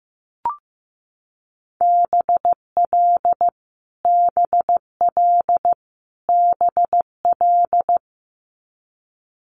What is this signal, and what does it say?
Morse code, heard as a keyed tone, about 15 wpm, 700 Hz. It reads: BL BL BL
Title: Multi-letter Phonemes Repeated 3x 15wpm